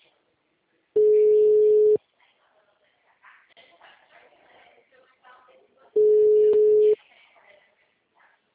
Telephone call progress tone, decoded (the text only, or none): ringback tone